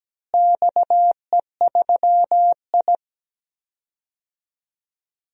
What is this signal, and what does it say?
Morse code, heard as a keyed tone, about 17 wpm, 700 Hz. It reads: XE3I
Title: DXCC-100 Call Signs - 4 Characters - Repeated 3x - 17wpm